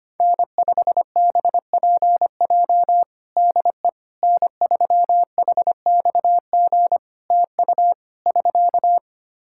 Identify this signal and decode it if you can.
Morse code, heard as a keyed tone, about 25 wpm, 700 Hz. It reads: N5BPJ DE N3HXG TU <SK>
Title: K1USN SST QSO Practice 25wpm